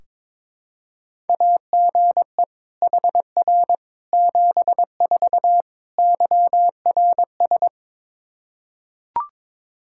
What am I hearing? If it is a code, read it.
Morse code, heard as a keyed tone, about 22 wpm, 700 Hz. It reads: AGE HR 74 YRS
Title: Practice QSO - Exchange 3 - More Info - 22wpm